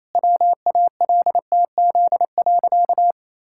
Morse code, heard as a keyed tone, about 28 wpm, 700 Hz. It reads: WALTZ.
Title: Pangram Challenge v2 - 28wpm